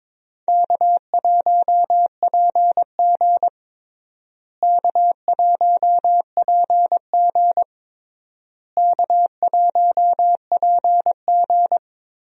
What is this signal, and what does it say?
Morse code, heard as a keyed tone, about 22 wpm, 700 Hz. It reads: K1PG K1PG K1PG
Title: USA Call Signs - 4 Characters - Repeated 3x - 22wpm